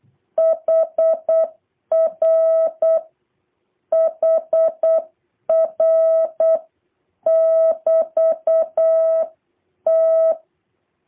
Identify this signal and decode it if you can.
Morse code, heard as a keyed tone, about 8 wpm, 650 Hz. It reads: HR HR <BT> T
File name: jh8cbh / 2023cw